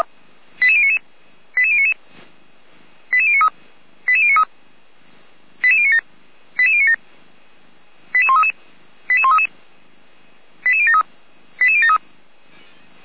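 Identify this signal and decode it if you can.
Five-tone selective call (ZVEI), heard as quick runs of five tones, tones of about 70 ms. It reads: 88099 88099 88093 88093 88097 88097 88133 88133 88073 88073
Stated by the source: ZVEI five-tone selective call received by SDR.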